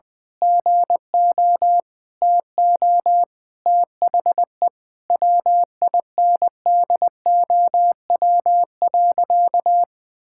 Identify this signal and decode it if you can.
Morse code, heard as a keyed tone, about 20 wpm, 700 Hz. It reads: GO TO THE WINDOW.
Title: Sentences from Top 1600 Words 20wpm